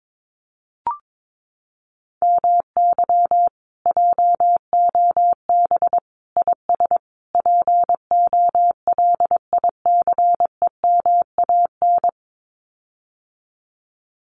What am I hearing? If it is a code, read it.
Morse code, heard as a keyed tone, about 22 wpm, 700 Hz. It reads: MY JOB IS POLICEMAN
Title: Common QSO Phrases Encore 22wpm